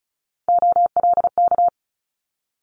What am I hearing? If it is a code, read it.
Morse code, heard as a keyed tone, about 36 wpm, 700 Hz. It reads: OLK
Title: Multi-letter Phonemes Quick Fire 35wpm